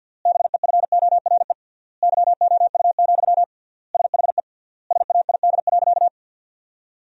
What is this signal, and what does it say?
Morse code, heard as a keyed tone, about 52 wpm, 700 Hz. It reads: BEFORE YOU, SHE SAID.
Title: 3x Word-Spacing - Sentences from Top 400 Words - 50wpm